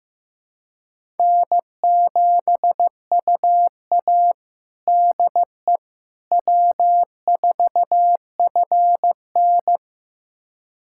Morse code, heard as a keyed tone, about 15 wpm, 700 Hz. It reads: N7UA DE W4FN
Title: USA Callsign DE Callsign - 4 Characters - 15wpm